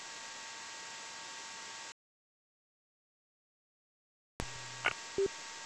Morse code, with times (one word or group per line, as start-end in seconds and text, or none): none